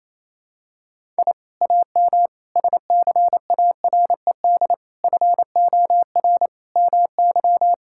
Morse code, heard as a keyed tone, about 28 wpm, 700 Hz. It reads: I AM SCARED FOR MY
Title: Sentences from Top 1000 Words - Review Part 5 - 28wpm